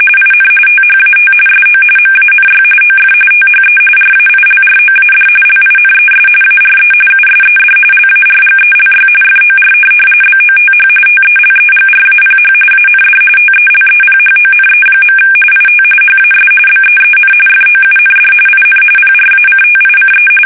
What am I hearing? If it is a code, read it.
RTTY, about 45.45 baud. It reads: IN THAT SO MANY OF THE RAILROAD LINES ARE USED LARGELY FOR BOTH TELEGRAPH AND TELEPHONE. IT HAS BEEN FOUND IN PRACTICE
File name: fsk850